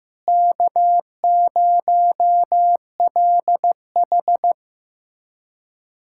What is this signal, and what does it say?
Morse code, heard as a keyed tone, about 15 wpm, 700 Hz. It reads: K0LH
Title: USA Callsign DE Callsign - 4 Characters - 15wpm